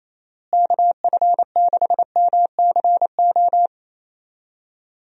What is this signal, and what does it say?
Morse code, heard as a keyed tone, about 28 wpm, 700 Hz. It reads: KF6MCO
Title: USA Call Signs - 6 Characters - Repeated 3x - 28wpm